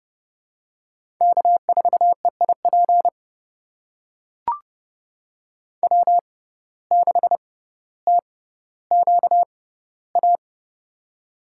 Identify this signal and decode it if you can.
Morse code, heard as a keyed tone, about 30 wpm, spaced as for 15 wpm, 700 Hz. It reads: K4EIP W6TQA
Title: USA Call Signs - 5 Characters - 6x Character Spacing - 30wpm